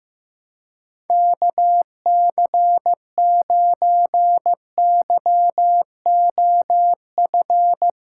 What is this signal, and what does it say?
Morse code, heard as a keyed tone, about 15 wpm, 700 Hz. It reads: KC9YOF
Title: USA Callsign DE Callsign 15wpm